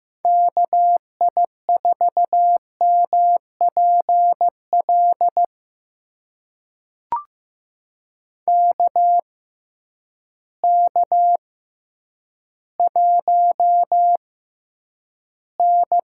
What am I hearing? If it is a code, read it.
Morse code, heard as a keyed tone, about 15 wpm, spaced as for 9 wpm, 700 Hz. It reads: KI4MPL KK1N
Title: USA Call Signs - 6x Character Spacing - 15wpm